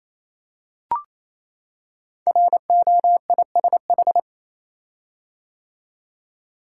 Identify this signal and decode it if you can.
Morse code, heard as a keyed tone, about 28 wpm, 700 Hz. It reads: ROISH
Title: Code Groups - 5 Letters - 28wpm